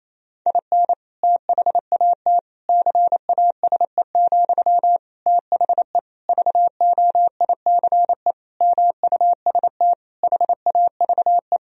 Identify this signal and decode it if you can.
Morse code, heard as a keyed tone, about 28 wpm, 700 Hz. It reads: IN THAT CASE, THE VOICE MUST HAVE
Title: Sentences from Top 1000 Words - Review Part 2 - 28wpm